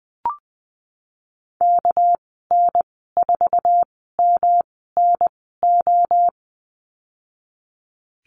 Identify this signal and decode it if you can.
Morse code, heard as a keyed tone, about 20 wpm, spaced as for 15 wpm, 700 Hz. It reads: KN4MNO
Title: 207 Lesson 33g - Letter j - Call Signs 20wpm